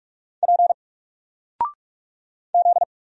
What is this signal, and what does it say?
Morse code, heard as a keyed tone, about 44 wpm, 700 Hz. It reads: P Z
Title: Single Character (Letter:Number:Punctuation:Prosign) 45wpm